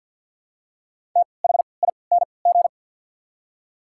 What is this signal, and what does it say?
Morse code, heard as a keyed tone, about 53 wpm, spaced as for 29 wpm, 690 Hz. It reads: THING
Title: 3x Character Spacing - Top 100 Words - 50wpm